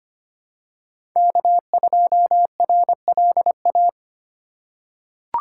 Morse code, heard as a keyed tone, about 25 wpm, 700 Hz. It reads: K2RLA
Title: Code Groups - 5 Letter:Numbers - 25wpm